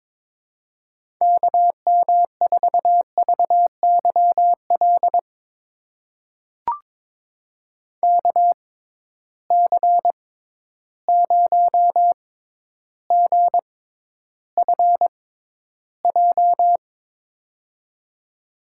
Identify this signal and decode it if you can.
Morse code, heard as a keyed tone, about 22 wpm, spaced as for 11 wpm, 700 Hz. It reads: KM4VYL KC0GFJ
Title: USA Call Signs - 6 Characters - 6x Character Spacing - 22wpm